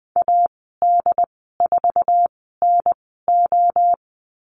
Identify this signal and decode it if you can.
Morse code, heard as a keyed tone, about 20 wpm, spaced as for 15 wpm, 700 Hz. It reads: AD4NO
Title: 179 Lesson 29g - Slash - Call Signs 20wpm